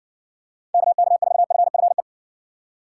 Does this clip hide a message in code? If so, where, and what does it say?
Yes: Morse code from 0.7 to 2.0 s, heard as a keyed tone, about 60 wpm, 700 Hz. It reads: KK4FFE